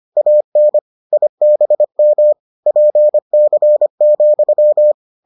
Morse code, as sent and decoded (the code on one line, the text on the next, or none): .- -. / .. -... -- / .--. -.-. --..--
AN IBM PC,